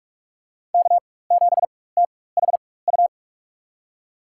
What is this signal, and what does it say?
Morse code, heard as a keyed tone, about 46 wpm, spaced as for 21 wpm, 690 Hz. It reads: K7THU
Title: USA Call Signs - 4x Character Spacing - 45wpm